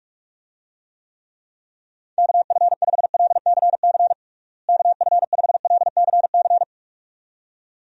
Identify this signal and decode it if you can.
Morse code, heard as a keyed tone, about 46 wpm, 700 Hz. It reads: KF5LCC KF5LCC
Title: USA Call Signs - 6 Characters - Repeated 2x - 45wpm